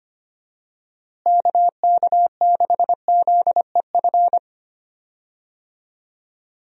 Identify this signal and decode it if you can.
Morse code, heard as a keyed tone, about 25 wpm, 700 Hz. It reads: KK6ZEF
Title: USA Call Signs - 6 Characters - Repeated 3x - 25wpm